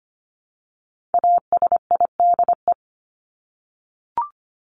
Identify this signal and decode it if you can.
Morse code, heard as a keyed tone, about 25 wpm, 700 Hz. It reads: ASIDE
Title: Top 1751-2000 Words 25wpm